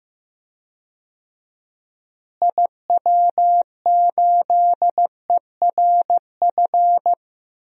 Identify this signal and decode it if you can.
Morse code, heard as a keyed tone, about 16 wpm, 700 Hz. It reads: IW8ERF
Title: DXCC-100 Call Signs - 6 Characters - Repeated 3x - 15wpm